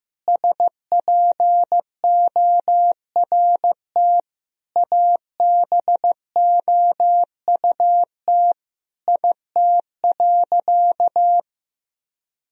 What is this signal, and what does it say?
Morse code, heard as a keyed tone, about 15 wpm, 700 Hz. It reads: SPORT ABOUT IT.
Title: Sentences from Top 1000 Words - Review Part 4 - 15wpm